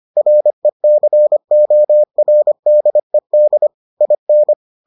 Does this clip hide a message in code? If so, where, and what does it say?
Yes: Morse code from 0.2 to 4.5 s, heard as a keyed tone, about 25 wpm, 600 Hz. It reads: RECORDED IN